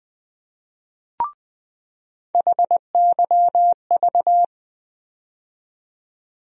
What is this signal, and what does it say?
Morse code, heard as a keyed tone, about 20 wpm, 700 Hz. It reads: HYV
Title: USA Call Sign Postfixes - 3 Letters 20wpm